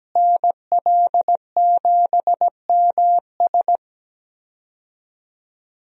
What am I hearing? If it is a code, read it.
Morse code, heard as a keyed tone, about 17 wpm, 700 Hz. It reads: NL7MS
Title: USA Call Signs - 5 Characters - Repeated 3x - 17wpm